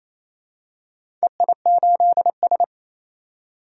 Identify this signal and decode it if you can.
Morse code, heard as a keyed tone, about 28 wpm, 700 Hz. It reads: EI8S
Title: DXCC-100 Call Signs - 4 Characters - 28wpm